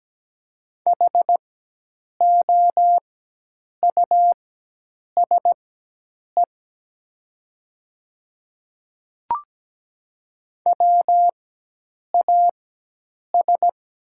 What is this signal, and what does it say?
Morse code, heard as a keyed tone, about 17 wpm, spaced as for 8 wpm, 700 Hz. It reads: HOUSE WAS
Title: 4x Character Spacing - Top 200 Words - 17wpm